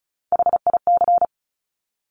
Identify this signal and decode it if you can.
Morse code, heard as a keyed tone, about 36 wpm, 700 Hz. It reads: HIC
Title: Top Three Letter Combinations in Words 35wpm